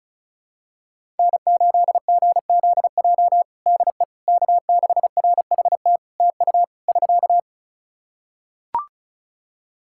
Morse code, heard as a keyed tone, about 36 wpm, 700 Hz. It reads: N8GZJ DE K6RHT TU <SK>
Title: Practice QSO - Exchange 4 - Ending - 35wpm